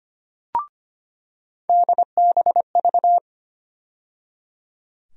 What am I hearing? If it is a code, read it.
Morse code, heard as a keyed tone, about 25 wpm, 700 Hz. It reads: DBV
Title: USA Call Sign Postfixes - 3 Letters 25wpm